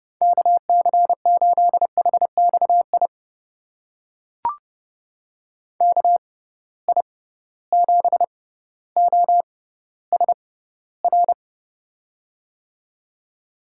Morse code, heard as a keyed tone, about 32 wpm, spaced as for 16 wpm, 700 Hz. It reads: KC8HXI KI7OSR
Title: USA Call Signs - 6 Characters - 6x Character Spacing - 30wpm